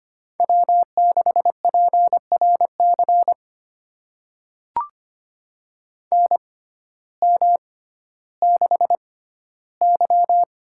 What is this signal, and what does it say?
Morse code, heard as a keyed tone, about 25 wpm, spaced as for 14 wpm, 700 Hz. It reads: W6PRC NM6Y
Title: USA Call Signs - 6x Character Spacing - 25wpm